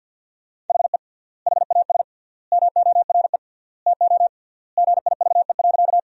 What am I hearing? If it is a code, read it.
Morse code, heard as a keyed tone, about 50 wpm, 700 Hz. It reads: HE HAS MORE TO GIVE.
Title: 3x Word-Spacing - Sentences from Top 200 Words - 50wpm